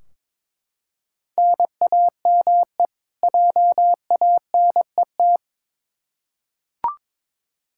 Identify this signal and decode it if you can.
Morse code, heard as a keyed tone, about 21 wpm, 700 Hz. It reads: NAME JANET
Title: K1USN SST QSO Practice 22wpm